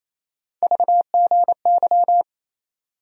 Morse code, heard as a keyed tone, about 28 wpm, 700 Hz. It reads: VGY
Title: USA Call Sign Postfixes - 3 Letters 28wpm